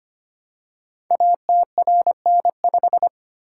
Morse code, heard as a keyed tone, about 25 wpm, 700 Hz. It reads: ATRN5
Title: Code Groups - 5 Letter:Numbers - 25wpm